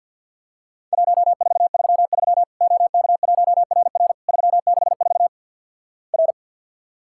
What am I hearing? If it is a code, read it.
Morse code, heard as a keyed tone, about 51 wpm, 700 Hz. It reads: 1433 OK1RR 364 R
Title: Worked All Europe DX Contest - QTC Exchange with Repeats - 50wpm